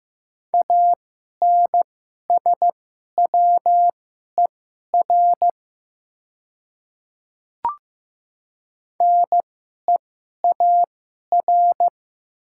Morse code, heard as a keyed tone, about 15 wpm, spaced as for 11 wpm, 700 Hz. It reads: ANSWER NEAR
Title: 2x Character Spacing - Top 300 Words - 15wpm